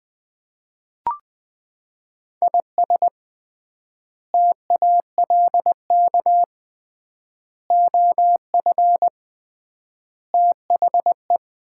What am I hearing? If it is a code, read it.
Morse code, heard as a keyed tone, about 20 wpm, 700 Hz. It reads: IS TALK OF THE
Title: 3x Word-Spacing - Sets of 5 Words - 20wpm